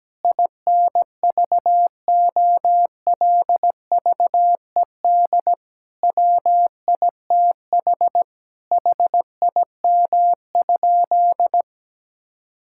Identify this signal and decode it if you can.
Morse code, heard as a keyed tone, about 17 wpm, 700 Hz. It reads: INVOLVED WITH HIM?
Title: Sentences from Top 1000 Words - Review Part 5 - 17wpm